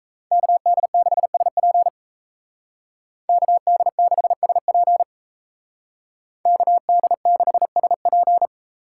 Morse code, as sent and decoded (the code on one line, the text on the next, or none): -.- -.. -.... ... .--. / -.- -.. -.... ... .--. / -.- -.. -.... ... .--.
KD6SP KD6SP KD6SP